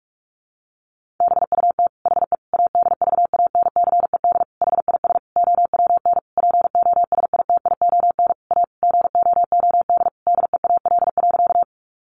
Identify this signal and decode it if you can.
Morse code, heard as a keyed tone, about 44 wpm, 700 Hz. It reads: BUT HE ADVANCED HIS OWN POSITION A GOOD DEAL.
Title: Sentences from Top 1200 Words 45wpm